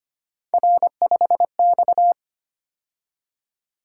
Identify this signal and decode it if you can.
Morse code, heard as a keyed tone, about 25 wpm, 700 Hz. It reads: R5X
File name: Three Letter-Number Combinations - 25wpm